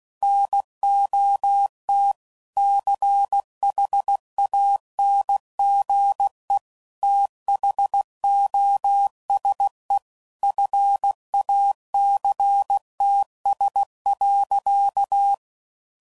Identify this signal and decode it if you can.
Morse code, heard as a keyed tone, about 16 wpm, 800 Hz. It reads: NOT CHANGE THOSE FACTS.